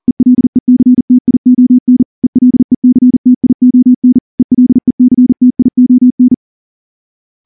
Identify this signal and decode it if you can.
Morse code, heard as a keyed tone, about 38 wpm, 250 Hz. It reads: ELECTION ELECTION ELECTION
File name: morse / morse-training-40WPM